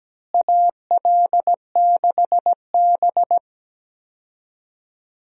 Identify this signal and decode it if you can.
Morse code, heard as a keyed tone, about 17 wpm, 700 Hz. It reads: AL6B
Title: USA Call Signs - 4 Characters - Repeated 3x - 17wpm